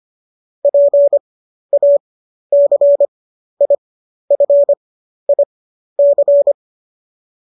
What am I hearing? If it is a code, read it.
Morse code, heard as a keyed tone, about 25 wpm, spaced as for 12 wpm, 570 Hz. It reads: PACIFIC
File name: qso008 12